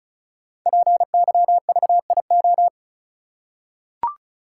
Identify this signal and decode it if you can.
Morse code, heard as a keyed tone, about 36 wpm, 700 Hz. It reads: PYVIO
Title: Code Groups - 5 Letter:Numbers - 35wpm